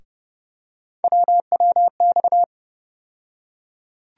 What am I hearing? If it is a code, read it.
Morse code, heard as a keyed tone, about 30 wpm, 700 Hz. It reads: WWX
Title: USA Call Sign Postfixes - 3 Letters 30wpm